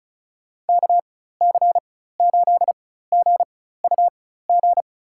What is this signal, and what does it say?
Morse code, heard as a keyed tone, about 34 wpm, spaced as for 16 wpm, 700 Hz. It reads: KC8GUG
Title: USA Call Signs - 6 Characters - 4x Character Spacing - 35wpm